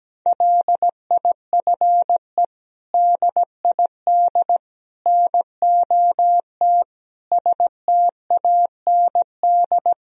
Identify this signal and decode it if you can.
Morse code, heard as a keyed tone, about 17 wpm, 700 Hz. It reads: LIFE DID NOT STAND
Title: Sentences from Top 1000 Words - Review Part 1 - 17wpm